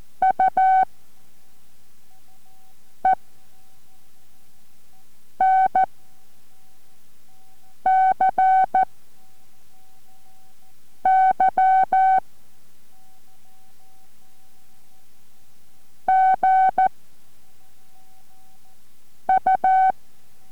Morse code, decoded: U E N C Y G U